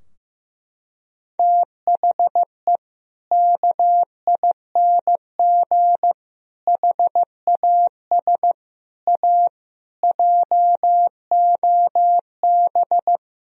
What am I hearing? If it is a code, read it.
Morse code, heard as a keyed tone, about 15 wpm, 700 Hz. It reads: THE KING HAS A JOB
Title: Sentences from Top 1000 Words - Review Part 3 - 15wpm